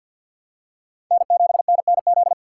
Morse code, heard as a keyed tone, about 53 wpm, 700 Hz. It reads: N7NNZ
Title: USA Call Signs - 5 Characters - 4x Character Spacing - 50wpm